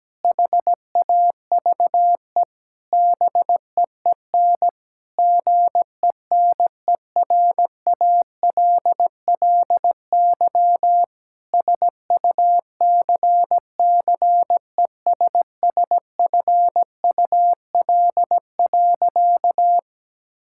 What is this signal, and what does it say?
Morse code, heard as a keyed tone, about 17 wpm, 700 Hz. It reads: HAVE BEEN GENERALLY SUCCESSFUL.
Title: Sentences from Top 1000 Words - Review Part 6 - 17wpm